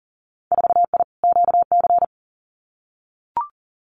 Morse code, heard as a keyed tone, about 39 wpm, 700 Hz. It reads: 4I QC
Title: Winter Field Day Exchanges Encore 40wpm